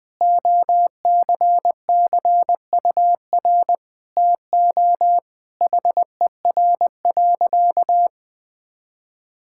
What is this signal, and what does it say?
Morse code, heard as a keyed tone, about 21 wpm, 700 Hz. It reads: OCCUR TO HER.
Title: Sentences from Top 1000 Words - Review Part 6 - 20wpm